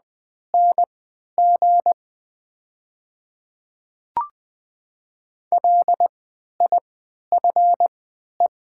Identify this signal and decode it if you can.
Morse code, heard as a keyed tone, about 21 wpm, spaced as for 12 wpm, 700 Hz. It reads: NG LIFE
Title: 3x Character Spacing - Top 300 Words - 20wpm